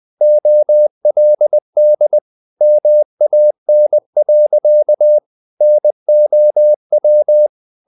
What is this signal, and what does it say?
Morse code, heard as a keyed tone, about 20 wpm, 600 Hz. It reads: OLD MAN. NOW
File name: the old man and the sea part 2 20wpm